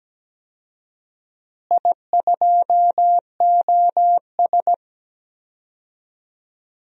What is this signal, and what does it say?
Morse code, heard as a keyed tone, about 17 wpm, 700 Hz. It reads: I2OS
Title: DXCC-100 Call Signs - 4 Characters - Repeated 3x - 17wpm